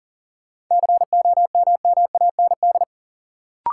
Morse code, heard as a keyed tone, about 40 wpm, 700 Hz. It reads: COMMAND
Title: Sentences from Top 1600 Words - New Word Drill Alt - 40wpm